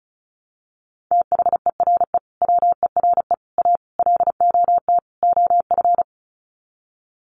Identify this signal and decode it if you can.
Morse code, heard as a keyed tone, about 34 wpm, 700 Hz. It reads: THERE WERE A LOT OF
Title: Sets of 5 Words 35wpm